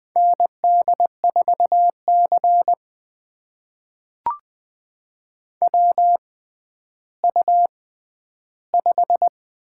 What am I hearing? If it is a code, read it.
Morse code, heard as a keyed tone, about 20 wpm, spaced as for 11 wpm, 700 Hz. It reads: ND4C WU5
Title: USA Call Signs - 4 Characters - 6x Character Spacing - 20wpm